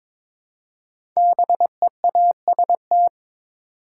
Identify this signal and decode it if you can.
Morse code, heard as a keyed tone, about 22 wpm, 700 Hz. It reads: BEAST
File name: 5 Letter Words - First 3 Letters Given - 22wpm